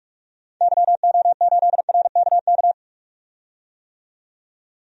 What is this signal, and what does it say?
Morse code, heard as a keyed tone, about 44 wpm, 700 Hz. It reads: YO8RKK
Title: DXCC-100 Call Signs - 6 Characters - 45wpm